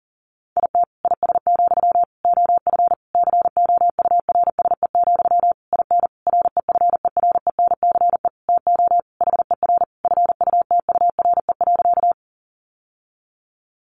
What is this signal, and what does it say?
Morse code, heard as a keyed tone, about 40 wpm, 700 Hz. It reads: IT IS, OF COURSE, IN REFERENCE TO HER FUTURE.
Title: Sentences from Top 1200 Words 40wpm